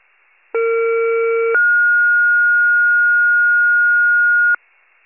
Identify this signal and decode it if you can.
Two-tone page, heard as a one-second tone, then a three-second tone, about 460 then 1430 Hz. It is heard from 0.5 to 4.6 s.
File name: pager / 21 FIRE / wcalert